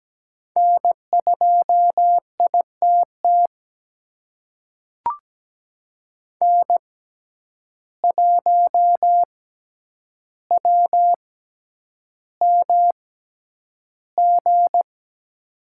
Morse code, heard as a keyed tone, about 17 wpm, spaced as for 9 wpm, 700 Hz. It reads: N2ITT N1WMG
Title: USA Call Signs - 5 Characters - 6x Character Spacing - 17wpm